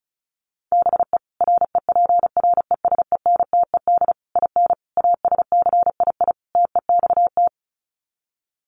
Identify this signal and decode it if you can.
Morse code, heard as a keyed tone, about 34 wpm, 700 Hz. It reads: BE REPRESENTED IN ASCII TEXT
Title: Sets of 5 Words 35wpm